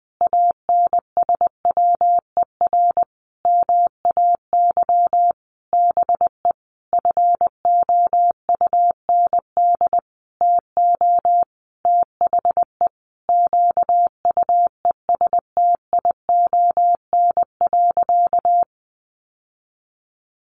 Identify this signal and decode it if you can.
Morse code, heard as a keyed tone, about 20 wpm, 700 Hz. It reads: ANSWER MAY BE FOUND TO THE QUESTION.
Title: Sentences from Top 1400 Words 20wpm